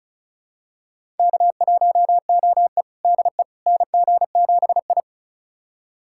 Morse code, heard as a keyed tone, about 34 wpm, 700 Hz. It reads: K1OE DE NG7I
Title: USA Callsign DE Callsign - 4 Characters - 35wpm